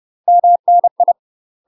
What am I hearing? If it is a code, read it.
Morse code, heard as a keyed tone, about 30 wpm, 700 Hz. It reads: MNI
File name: UK MASTER QSOs TRAINING/30WPM